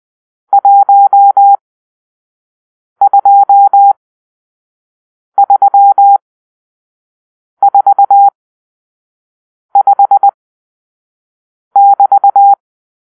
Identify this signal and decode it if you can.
Morse code, heard as a keyed tone, about 20 wpm, spaced as for 5 wpm, 800 Hz. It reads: 12345<BT>
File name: arrl / Grp-6 Drill-2